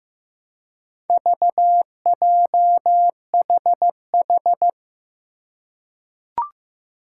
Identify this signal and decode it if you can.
Morse code, heard as a keyed tone, about 16 wpm, 700 Hz. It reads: VJHH
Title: Four Challenging Letter-Number Combinations 15wpm